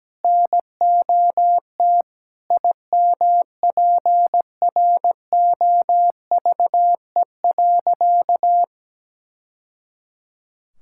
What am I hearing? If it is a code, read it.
Morse code, heard as a keyed tone, about 17 wpm, 700 Hz. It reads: NOT IMPROVE.
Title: Sentences from Top 1000 Words - Review Part 5 - 17wpm